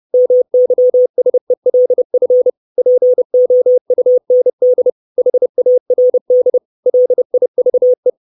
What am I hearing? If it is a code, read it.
Morse code, heard as a keyed tone, about 30 wpm, 500 Hz. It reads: MYSELF POUND HARD LIVE
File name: 632 Words 30WPM